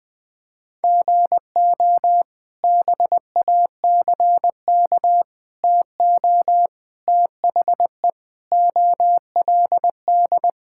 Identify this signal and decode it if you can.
Morse code, heard as a keyed tone, about 20 wpm, 700 Hz. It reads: GO BACK TO THE OLD
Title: Sentences from Top 1000 Words - Review Part 5 - 20wpm